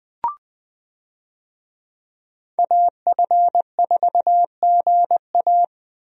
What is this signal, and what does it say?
Morse code, heard as a keyed tone, about 20 wpm, 700 Hz. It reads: AF4GA
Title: USA Call Signs - 5 Characters - Repeated 2x - 20wpm